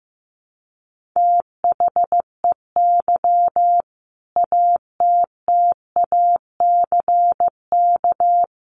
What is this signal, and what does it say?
Morse code, heard as a keyed tone, about 15 wpm, 700 Hz. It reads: THEY ATTACK
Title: Sentences from Top 1500 Words 15wpm